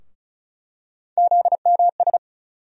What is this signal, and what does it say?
Morse code, heard as a keyed tone, about 34 wpm, 700 Hz. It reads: ZMS
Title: USA Call Sign Postfixes - 3 Letters 35wpm